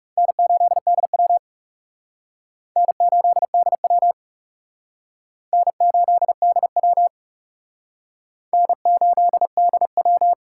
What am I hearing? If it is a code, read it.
Morse code, heard as a keyed tone, about 38 wpm, 700 Hz. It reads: N8DW N8DW N8DW N8DW